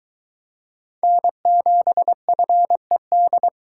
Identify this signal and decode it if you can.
Morse code, heard as a keyed tone, about 22 wpm, 700 Hz. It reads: N7FED